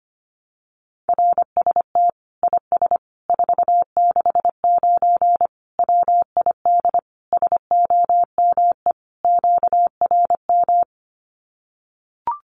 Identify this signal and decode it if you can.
Morse code, heard as a keyed tone, about 25 wpm, 700 Hz. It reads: RST IS 469 WID SOME QRM
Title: Common QSO Phrases 25wpm